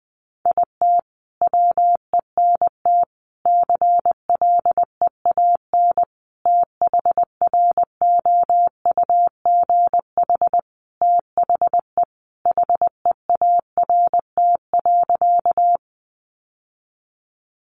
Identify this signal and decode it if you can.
Morse code, heard as a keyed tone, about 20 wpm, 700 Hz. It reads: IT WENT CLEAN THROUGH THE HEART.
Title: Sentences from Top 1200 Words 20wpm